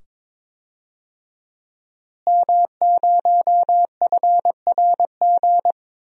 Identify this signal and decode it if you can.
Morse code, heard as a keyed tone, about 21 wpm, 700 Hz. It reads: M0FRG
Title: DXCC-100 Call Signs - 5 Characters - Repeated 2x - 22wpm